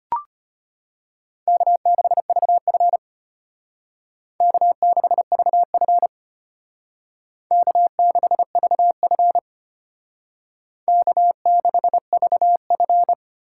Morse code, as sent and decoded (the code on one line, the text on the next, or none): -.- -.... ...- ..-. / -.- -.... ...- ..-. / -.- -.... ...- ..-. / -.- -.... ...- ..-.
K6VF K6VF K6VF K6VF